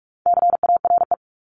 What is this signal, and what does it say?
Morse code, heard as a keyed tone, about 44 wpm, 700 Hz. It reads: CARE
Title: Top 500 Words 45wpm